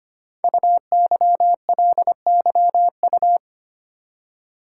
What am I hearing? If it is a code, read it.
Morse code, heard as a keyed tone, about 25 wpm, 700 Hz. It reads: UYLYU
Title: Code Groups - 5 Letters - 25wpm